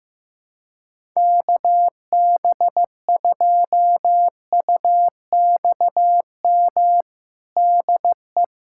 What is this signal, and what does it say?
Morse code, heard as a keyed tone, about 15 wpm, 700 Hz. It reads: KB2UXM DE
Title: USA Callsign DE Callsign - 6 Characters - 15wpm